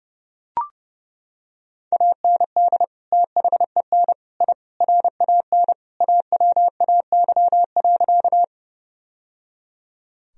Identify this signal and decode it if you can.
Morse code, heard as a keyed tone, about 30 wpm, 700 Hz. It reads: AND THEN I RAN AWAY.
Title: Sentences from Top 1000 Words - Review Part 3 - 30wpm